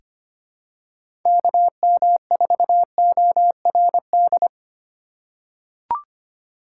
Morse code, heard as a keyed tone, about 25 wpm, 700 Hz. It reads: KM4ORD
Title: K1USN SST QSO Practice - 25wpm